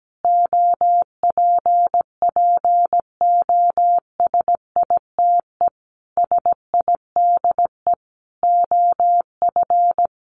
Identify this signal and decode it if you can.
Morse code, heard as a keyed tone, about 17 wpm, 700 Hz. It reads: OPPOSITE SIDE OF
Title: Sets of 5 Words Encore2 17wpm